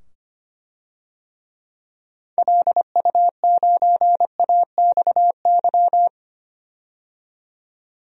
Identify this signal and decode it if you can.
Morse code, heard as a keyed tone, about 25 wpm, 700 Hz. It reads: LU9AXY
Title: DXCC-100 Call Signs - 6 Characters - 25wpm